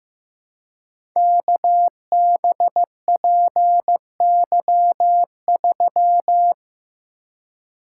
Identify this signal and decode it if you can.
Morse code, heard as a keyed tone, about 15 wpm, 700 Hz. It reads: KBPY3
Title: Code Groups - 5 Letter:Numbers - 15wpm